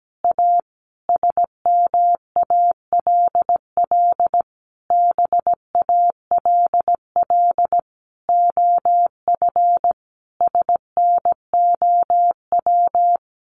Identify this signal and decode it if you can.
Morse code, heard as a keyed tone, about 17 wpm, 700 Hz. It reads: A SMALL BALL OF SNOW
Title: Easy Phrases 17wpm